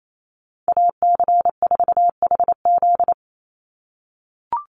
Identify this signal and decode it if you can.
Morse code, heard as a keyed tone, about 28 wpm, 700 Hz. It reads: AC4HZ
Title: USA Call Signs 28wpm